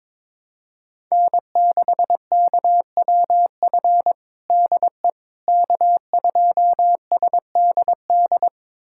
Morse code, heard as a keyed tone, about 21 wpm, 700 Hz. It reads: N6KWF DE K2SDD
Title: USA Callsign DE Callsign - 5 Characters - 22wpm